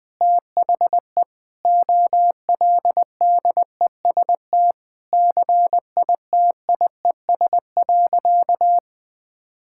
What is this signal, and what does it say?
Morse code, heard as a keyed tone, about 20 wpm, 700 Hz. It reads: THE OLDEST CITIES.
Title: Sentences with 5-Word Phrases - Spoken Beginning - 20wpm